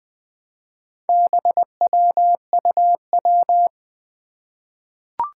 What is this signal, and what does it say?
Morse code, heard as a keyed tone, about 20 wpm, 700 Hz. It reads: BWUW
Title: Four Challenging Letter-Number Combinations 20wpm